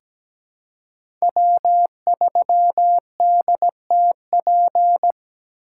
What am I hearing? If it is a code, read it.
Morse code, heard as a keyed tone, about 17 wpm, 700 Hz. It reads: W3DTP
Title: USA Call Signs - 5 Characters - 3x Character Spacing - 17wpm